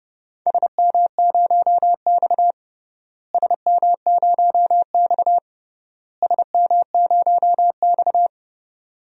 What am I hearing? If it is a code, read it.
Morse code, heard as a keyed tone, about 30 wpm, 700 Hz. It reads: SM0X SM0X SM0X
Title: DXCC-100 Call Signs - 4 Characters - Repeated 3x - 30wpm